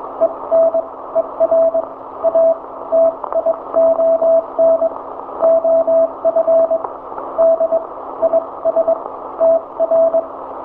Morse code, heard as a keyed tone, about 22 wpm, spaced as for 17 wpm, 660 Hz. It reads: ENERATION OF DISTR
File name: hamradio / last 500khz cw